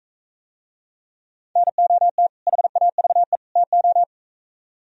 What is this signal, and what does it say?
Morse code, heard as a keyed tone, about 43 wpm, 700 Hz. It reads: NOT HAVE TO